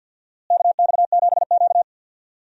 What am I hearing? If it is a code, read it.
Morse code, heard as a keyed tone, about 50 wpm, 700 Hz. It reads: KX7Q
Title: USA Call Signs - 4 Characters - Repeated 3x - 50wpm